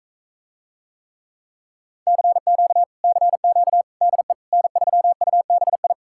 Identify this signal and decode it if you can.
Morse code, heard as a keyed tone, about 46 wpm, 700 Hz. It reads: CQ CQ DE N3UBI